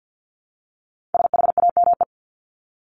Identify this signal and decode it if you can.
Morse code, heard as a keyed tone, about 50 wpm, 700 Hz. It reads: SHAME
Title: Top 1751-2000 Words 50wpm